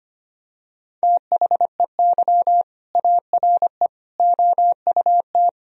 Morse code, heard as a keyed tone, about 25 wpm, 700 Hz. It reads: THEY ARE OUT
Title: Sentences from Top 1000 Words - Review Part 2 - 25wpm